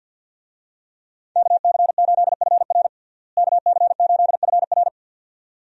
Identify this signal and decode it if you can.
Morse code, heard as a keyed tone, about 49 wpm, 700 Hz. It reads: KC8FR KC8FR
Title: USA Call Signs - 5 Characters - Repeated 2x - 50wpm